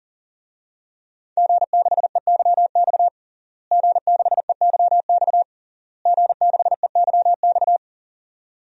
Morse code, heard as a keyed tone, about 40 wpm, 700 Hz. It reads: G6EYX G6EYX G6EYX
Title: DXCC-100 Call Signs - 5 Characters - Repeated 3x - 40wpm